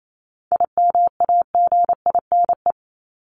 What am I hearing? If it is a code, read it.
Morse code, heard as a keyed tone, about 28 wpm, 700 Hz. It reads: IMAGINE
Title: Top 1251-1500 Words 28wpm